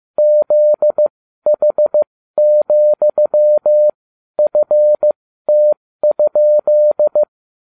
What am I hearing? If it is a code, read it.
Morse code, heard as a keyed tone, about 15 wpm, spaced as for 12 wpm, 600 Hz. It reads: ZH,FT?